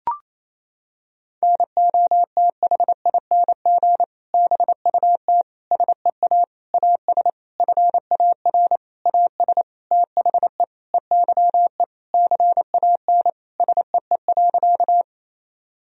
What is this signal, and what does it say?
Morse code, heard as a keyed tone, about 28 wpm, 700 Hz. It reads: NOTHING BUT SEA AS FAR AS THE EYE CAN SEE.
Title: Sentences from Top 1000 Words - Review Part 1 - 28wpm